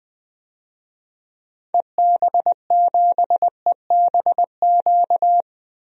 Morse code, heard as a keyed tone, about 20 wpm, 700 Hz. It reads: EB7EBQ
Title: DXCC-100 Call Signs - 6 Characters - Repeated 3x - 20wpm